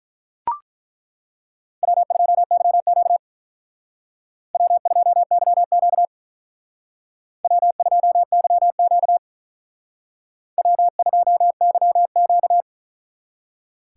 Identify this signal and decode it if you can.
Morse code, heard as a keyed tone, about 41 wpm, 700 Hz. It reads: W2YQ W2YQ W2YQ W2YQ
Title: Speed Racing - USA 1x2 Call Signs 35wpm